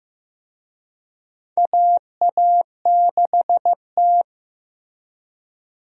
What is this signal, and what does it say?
Morse code, heard as a keyed tone, about 16 wpm, 700 Hz. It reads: AA6T
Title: USA Call Signs - 4 Characters - Repeated 2x - 15wpm